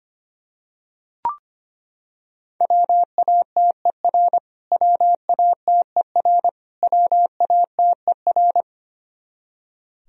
Morse code, heard as a keyed tone, about 25 wpm, 700 Hz. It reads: WATER WATER WATER
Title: Top 100 Words Repeated 25wpm